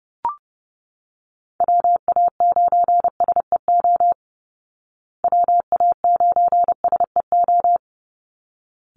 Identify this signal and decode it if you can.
Morse code, heard as a keyed tone, about 30 wpm, 700 Hz. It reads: WA9SEO WA9SEO
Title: Call Signs Repeated 3x - 30wpm